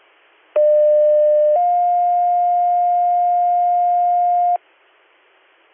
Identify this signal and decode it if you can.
Two-tone page, heard as a one-second tone, then a three-second tone, about 600 then 710 Hz. It is heard from 0.6 to 4.6 s.